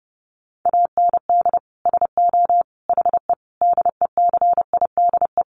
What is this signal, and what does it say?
Morse code, heard as a keyed tone, about 30 wpm, 700 Hz. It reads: AND SO HE DECIDE
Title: Sentences from Top 500 Words 30wpm